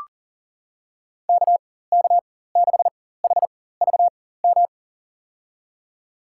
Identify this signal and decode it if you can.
Morse code, heard as a keyed tone, about 40 wpm, spaced as for 19 wpm, 700 Hz. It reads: KK6HVM
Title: USA Call Signs - 6 Characters - 4x Character Spacing - 40wpm